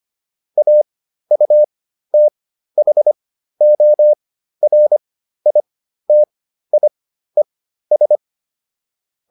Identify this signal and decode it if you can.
Morse code, heard as a keyed tone, about 25 wpm, spaced as for 13 wpm, 600 Hz. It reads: AUTHORITIES